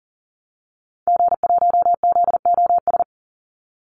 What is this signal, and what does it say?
Morse code, heard as a keyed tone, about 40 wpm, 700 Hz. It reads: G1ZOS
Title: Call Signs Repeated 2x - 40wpm